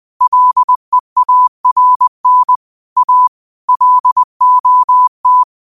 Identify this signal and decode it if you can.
Morse code, heard as a keyed tone, about 20 wpm, 1000 Hz. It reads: LEARN A LOT
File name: qotdfiles / 20wpm